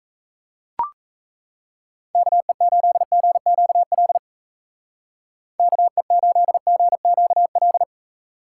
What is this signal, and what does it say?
Morse code, heard as a keyed tone, about 34 wpm, 700 Hz. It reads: KE8GQL KE8GQL
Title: Speed Racing - USA Call Signs 28wpm